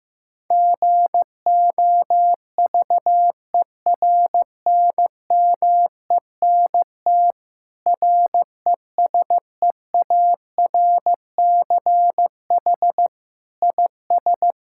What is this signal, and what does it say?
Morse code, heard as a keyed tone, about 15 wpm, 700 Hz. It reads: GOVERNMENT RESEARCH IS
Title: Sentences from Top 1000 Words - Review Part 6 - 15wpm